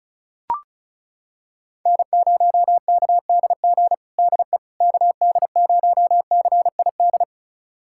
Morse code, heard as a keyed tone, about 34 wpm, 700 Hz. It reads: N0KDG DE KD0CID
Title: USA Callsign DE Callsign 35wpm